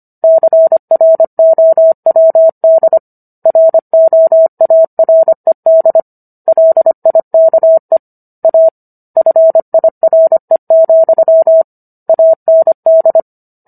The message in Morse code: CROWD ROARED LIKE A FIRE, AND

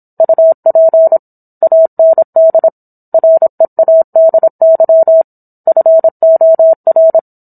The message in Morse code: UP AND READY FOR

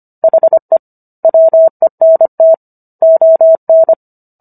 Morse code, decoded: HE WENT ON

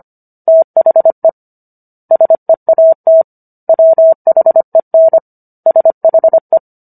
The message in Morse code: THE SEAT WHEN SHE

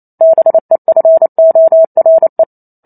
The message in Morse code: BEFORE

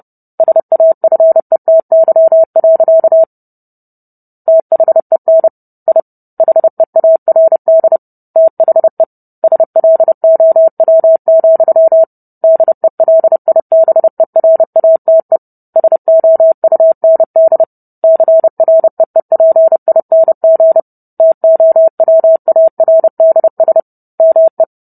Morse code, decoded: SAFETY. THEN I HEARD THE SLOW, DELIBERATE SOUND CREEPING TOWARDS ME

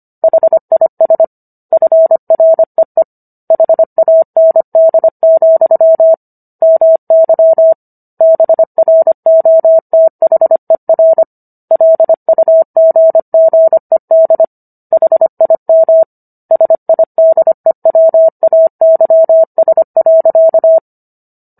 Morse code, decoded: HIS FREE HAND, MY BROTHER LUGGED HIM SIDEWAYS.

